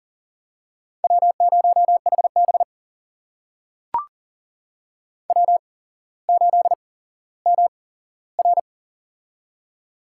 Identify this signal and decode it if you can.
Morse code, heard as a keyed tone, about 43 wpm, spaced as for 17 wpm, 700 Hz. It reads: W0HB W8MR
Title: USA Call Signs - 4 Characters - 8x Character Spacing - 40wpm